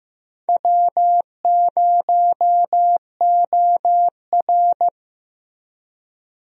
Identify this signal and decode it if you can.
Morse code, heard as a keyed tone, about 15 wpm, 700 Hz. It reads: W0OR
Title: USA Call Signs - 4 Characters - Repeated 3x - 15wpm